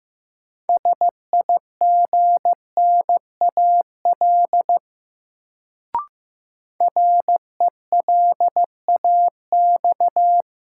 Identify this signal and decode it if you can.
Morse code, heard as a keyed tone, about 15 wpm, 700 Hz. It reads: SIGNAL RELAX
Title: Sentences from Top 1400 Words - New Word Drill - 15wpm